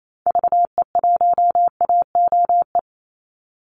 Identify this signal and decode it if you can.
Morse code, heard as a keyed tone, about 28 wpm, 700 Hz. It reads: VE1AOE
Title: Call Signs - 28wpm